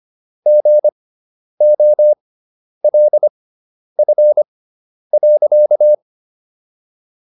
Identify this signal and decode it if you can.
Morse code, heard as a keyed tone, about 25 wpm, spaced as for 10 wpm, 600 Hz. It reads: GOLF.